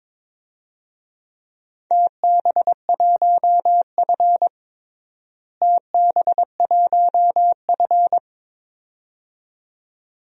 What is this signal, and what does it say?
Morse code, heard as a keyed tone, about 22 wpm, 700 Hz. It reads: TB1F TB1F
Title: DXCC-100 Call Signs - 4 Characters - Repeated 2x - 22wpm